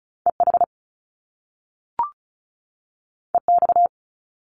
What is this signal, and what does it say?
Morse code, heard as a keyed tone, about 36 wpm, 700 Hz. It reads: EH EX
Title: Top Two Letter Combinations in Words 35wpm